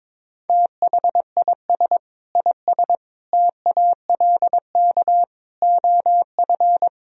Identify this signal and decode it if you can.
Morse code, heard as a keyed tone, about 22 wpm, 700 Hz. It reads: THIS IS TALK OF
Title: Speed-Racing - Sets of 5 Words 22wpm